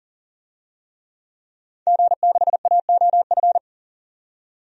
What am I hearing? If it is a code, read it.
Morse code, heard as a keyed tone, about 40 wpm, 700 Hz. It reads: G6AOF
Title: DXCC-100 Call Signs - 5 Characters - 40wpm